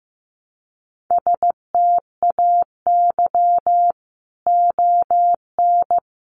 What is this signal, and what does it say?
Morse code, heard as a keyed tone, about 15 wpm, 700 Hz. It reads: STAY ON
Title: A Sound Of Thunder 15wpm